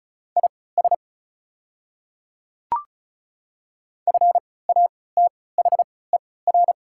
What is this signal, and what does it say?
Morse code, heard as a keyed tone, about 36 wpm, spaced as for 20 wpm, 690 Hz. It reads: IS FATHER
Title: 3x Character Spacing - Top 200 Words - 35wpm